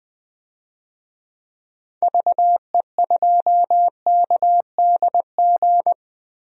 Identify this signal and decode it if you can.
Morse code, heard as a keyed tone, about 20 wpm, 700 Hz. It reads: VE2KDG
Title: DXCC-100 Call Signs - 6 Characters - 20wpm